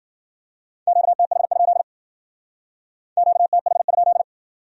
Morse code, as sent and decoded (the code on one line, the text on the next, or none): --.- - .... ..--.. / --.- - .... ..--..
QTH? QTH?